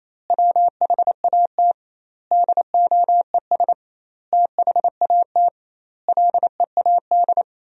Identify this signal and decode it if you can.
Morse code, heard as a keyed tone, about 28 wpm, 700 Hz. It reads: WHAT DOES THAT LEAD
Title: 2x Word-Spacing - Sentences from Top 500 Words - 28wpm